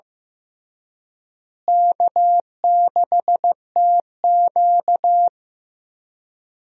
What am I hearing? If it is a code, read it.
Morse code, heard as a keyed tone, about 15 wpm, 700 Hz. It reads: K6TQ
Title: USA Call Signs - 4 Characters - Repeated 3x - 15wpm